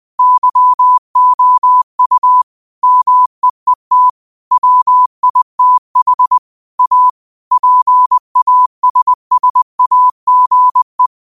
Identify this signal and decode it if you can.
Morse code, heard as a keyed tone, about 20 wpm, 1000 Hz. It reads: YOU MEET WITH A PASSAGE